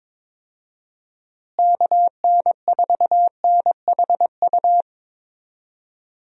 Morse code, heard as a keyed tone, about 22 wpm, 700 Hz. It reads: KN4NHU
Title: USA Call Signs - 6 Characters - Repeated 2x - 22wpm